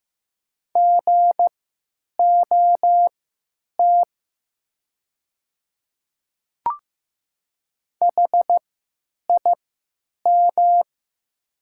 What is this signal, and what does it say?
Morse code, heard as a keyed tone, about 15 wpm, spaced as for 9 wpm, 700 Hz. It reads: GOT HIM
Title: 3x Character Spacing - Top 300 Words - 15wpm